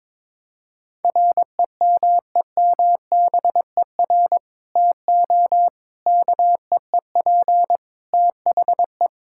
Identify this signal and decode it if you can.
Morse code, heard as a keyed tone, about 22 wpm, 700 Hz. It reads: REMEMBER TO KEEP THE
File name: practice / 22wpm / Sentences with 3-Word Phrases - 22wpm